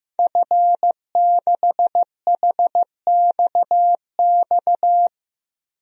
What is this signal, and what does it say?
Morse code, heard as a keyed tone, about 15 wpm, 700 Hz. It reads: F6HXX
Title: DXCC-100 Call Signs - 5 Characters - 15wpm